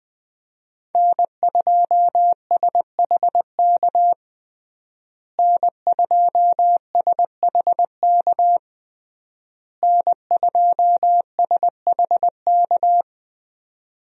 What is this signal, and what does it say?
Morse code, heard as a keyed tone, about 20 wpm, 700 Hz. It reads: N2SHK N2SHK N2SHK
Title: USA Call Signs - 5 Characters - Repeated 3x - 20wpm